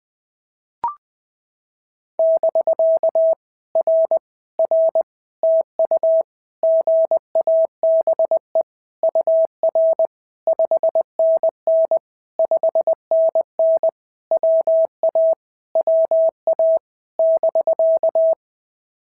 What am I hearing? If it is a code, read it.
Morse code, heard as a keyed tone, about 18 wpm, 650 Hz. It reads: <BK> R R TU GABE UR 5NN 5NN WA WA <BK>
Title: POTA QSO Practice - 20wpm